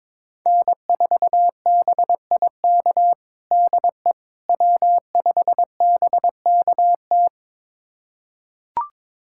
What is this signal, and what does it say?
Morse code, heard as a keyed tone, about 22 wpm, 700 Hz. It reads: N4BIK DE W5BKT
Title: USA Callsign DE Callsign - 5 Characters - 22wpm